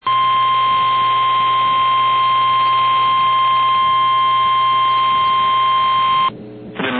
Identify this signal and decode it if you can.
Emergency Alert System (SAME) tones (the 1050 Hz attention tone), with noise. It is heard from 0.0 to 6.3 s.
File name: eas / KNCYKBIE / KNCYKBIE